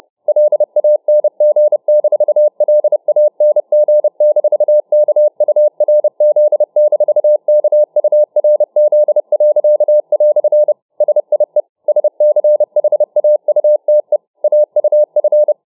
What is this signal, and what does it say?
Morse code, heard as a keyed tone, about 29 wpm, 600 Hz. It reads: LANG-LANG-KURZ-KURZ." SIE SCHAUTE AUF